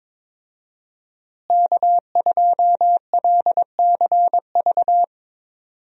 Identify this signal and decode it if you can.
Morse code, heard as a keyed tone, about 22 wpm, 700 Hz. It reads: K2LCV
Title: USA Call Signs - 5 Characters - Repeated 2x - 22wpm